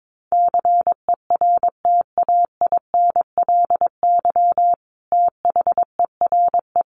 Morse code, heard as a keyed tone, about 22 wpm, 700 Hz. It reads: CERTAINLY THERE
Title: Sentences from Top 900 Words 22wpm